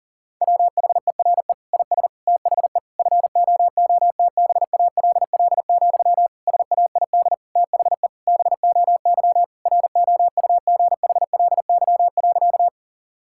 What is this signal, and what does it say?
Morse code, heard as a keyed tone, about 40 wpm, 700 Hz. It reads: WHERE IS THE FOOTBALL, SAID THE BOY ROUGHLY.
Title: Sentences from Top 1600 Words 40wpm